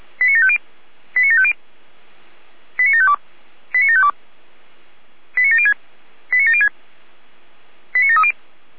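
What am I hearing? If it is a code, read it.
Five-tone selective call (ZVEI), heard as quick runs of five tones, tones of about 80 ms. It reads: 89750 89750 89752 89752 89706 89706 89730
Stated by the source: ZVEI five-tone selective call received on SDR.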